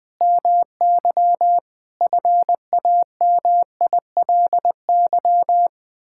Morse code, heard as a keyed tone, about 20 wpm, 700 Hz. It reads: MY FAMILY
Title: Sentences from Top 1000 Words - Review Part 6 - 20wpm